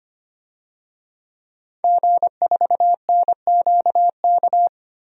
Morse code, heard as a keyed tone, about 25 wpm, 700 Hz. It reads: G4NQK
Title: DXCC-100 Call Signs - 5 Characters - 25wpm